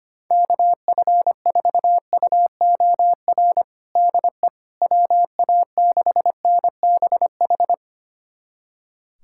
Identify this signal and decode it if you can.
Morse code, heard as a keyed tone, about 25 wpm, 700 Hz. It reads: KF4UOR DE WA6NBH
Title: USA Callsign DE Callsign - 6 Characters - 25wpm